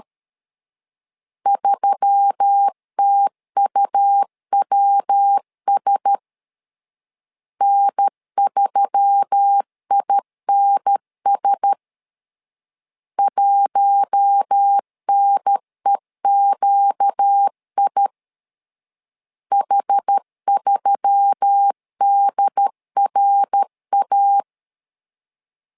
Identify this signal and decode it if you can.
Morse code, heard as a keyed tone, about 13 wpm, 800 Hz. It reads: 3TUWS N3INS 1NEQI H3DRA
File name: LEZIONI / 4 Q1S3 12wpm 2023